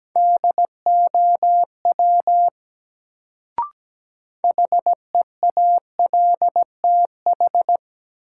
Morse code, heard as a keyed tone, about 17 wpm, 700 Hz. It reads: DOW HEALTH
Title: Sets of 5 Words - New Word Drill - 17wpm